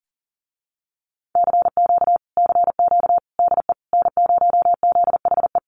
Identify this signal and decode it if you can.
Morse code, heard as a keyed tone, about 37 wpm, 700 Hz. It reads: CQ CQ DE N0ZHE
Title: Common QSO Phrases Encore 40wpm